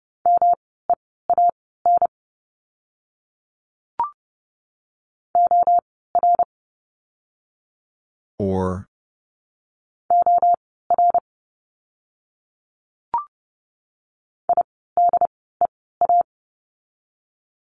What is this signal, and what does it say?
Morse code, heard as a keyed tone, about 30 wpm, spaced as for 17 wpm, 700 Hz. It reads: MEAN OR OR IDEA
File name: course / 30:10wpm / 108 Lesson 18f - Letter m - Words 30wpm